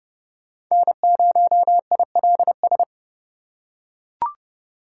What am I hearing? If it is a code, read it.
Morse code, heard as a keyed tone, about 31 wpm, 690 Hz. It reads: N0ILS
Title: USA Call Signs - 4x Character Spacing - 30wpm